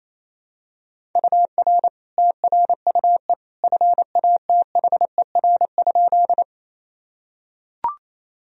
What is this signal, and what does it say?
Morse code, heard as a keyed tone, about 28 wpm, 700 Hz. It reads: UR TRUE FATHER?
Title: Sentences from Top 400 Words and Abbreviations 28wpm